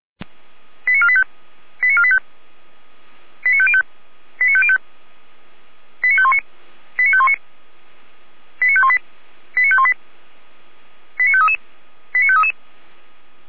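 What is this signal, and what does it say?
Five-tone selective call (ZVEI), heard as quick runs of five tones, tones of about 70 ms. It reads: 89485 89485 89505 89505 89519 89519 89518 89518 89533 89533
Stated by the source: ZVEI five-tone selective call 89456, received by SDR.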